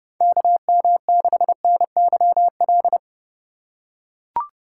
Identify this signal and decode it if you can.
Morse code, heard as a keyed tone, about 30 wpm, 700 Hz. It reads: KM6NYL
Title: USA Call Signs - 6 Characters - 2x Character Spacing - 30wpm